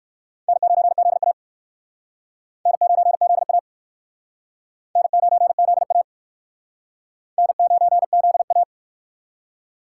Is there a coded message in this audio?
Yes: Morse code from 0.5 to 8.6 s, heard as a keyed tone, about 59 wpm, 700 Hz. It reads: N9ZA N9ZA N9ZA N9ZA